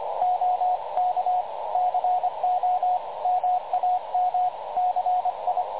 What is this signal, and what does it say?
Morse code, heard as a keyed tone, about 25 wpm, 730 Hz, with radio noise. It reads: OK CO MAM CI
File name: dlazaawansowanych / nagraniazpasma / websdr recording 3565.0kHz